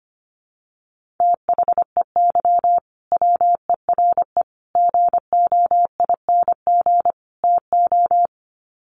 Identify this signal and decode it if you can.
Morse code, heard as a keyed tone, about 25 wpm, 700 Hz. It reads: THEY WERE GOING TO
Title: Sets of 4 Words 25wpm